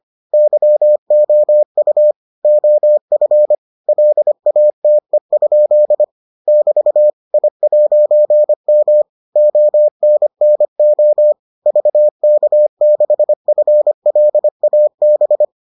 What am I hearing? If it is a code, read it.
Morse code, heard as a keyed tone, about 25 wpm, 600 Hz. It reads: YOU OF LATE? <BT> I'M ONNO VK6FLAB